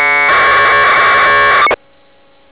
Packet radio (AX.25, as in APRS).